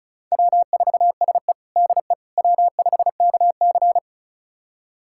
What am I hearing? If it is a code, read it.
Morse code, heard as a keyed tone, about 34 wpm, 700 Hz. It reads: W4SE DE W5KC
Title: USA Callsign DE Callsign - 4 Characters - 35wpm